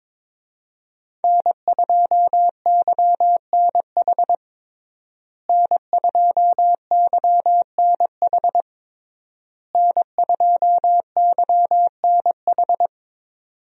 Morse code, heard as a keyed tone, about 22 wpm, 700 Hz. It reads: N2YNH N2YNH N2YNH
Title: USA Call Signs - 5 Characters - Repeated 3x - 22wpm